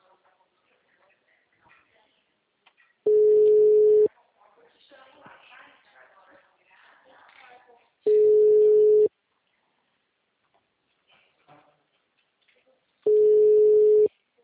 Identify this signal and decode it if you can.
Ringback tone (Europe, 425 Hz), with speech.
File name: out- -2001- - - .68310